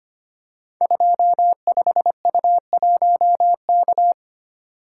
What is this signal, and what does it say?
Morse code, heard as a keyed tone, about 25 wpm, 700 Hz. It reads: 25U1K
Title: Code Groups - 5 Letter:Numbers - 25wpm